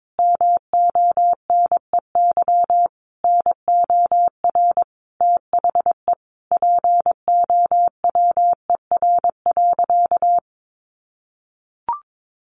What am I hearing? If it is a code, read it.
Morse code, heard as a keyed tone, about 22 wpm, 700 Hz. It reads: MONEY NOR THE POWER.
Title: Sentences from Top 1500 Words 22wpm